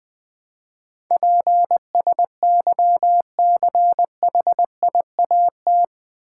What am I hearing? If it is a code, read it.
Morse code, heard as a keyed tone, about 20 wpm, 700 Hz. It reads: PSYCHIAT
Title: Build A Word - 4 Letters At A Time - 11-Letter Words - 20wpm